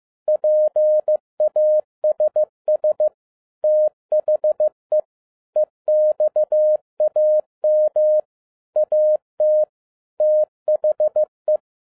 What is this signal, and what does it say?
Morse code, heard as a keyed tone, about 15 wpm, 610 Hz. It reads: PASS THE EXAM AT THE
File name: MY CW LIFE 1 15wpm